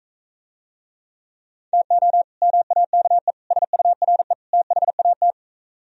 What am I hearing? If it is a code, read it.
Morse code, heard as a keyed tone, about 43 wpm, 700 Hz. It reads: TO MAKE SURE THAT